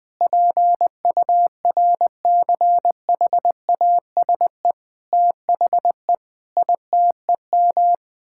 Morse code, heard as a keyed tone, about 20 wpm, 700 Hz. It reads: PURCHASE THE ITEM
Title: Sentences from Top 1000 Words - Review Part 6 - 20wpm